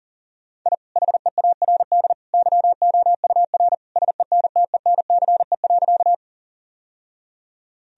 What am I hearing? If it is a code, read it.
Morse code, heard as a keyed tone, about 37 wpm, 700 Hz. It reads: I HEARD YOUR SENTENCE.
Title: Sentences from Top 1000 Words - Review Part 1 - 40wpm